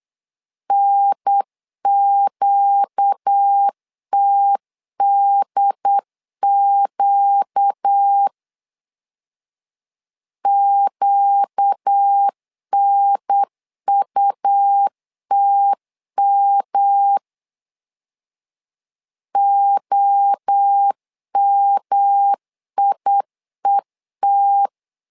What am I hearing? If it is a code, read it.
Morse code, heard as a keyed tone, about 8 wpm, 800 Hz. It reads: NQTDQ QNUTM OMIET
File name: 4 Q 8wpm